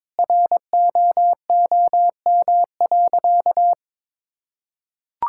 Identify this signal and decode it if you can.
Morse code, heard as a keyed tone, about 22 wpm, 700 Hz. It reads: ROOM.
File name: practice / 22wpm / Sentences with 2-Word Phrases - 22wpm